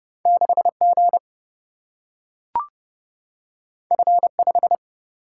Morse code, heard as a keyed tone, about 30 wpm, 700 Hz. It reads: 6G F5
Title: Single-Letter Single-Number 30wpm